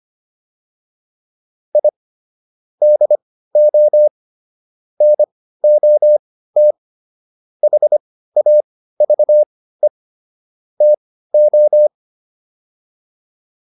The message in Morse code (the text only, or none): I DO NOT HAVE TO